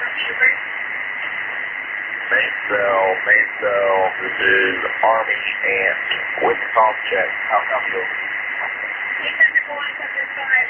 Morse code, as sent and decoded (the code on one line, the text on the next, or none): none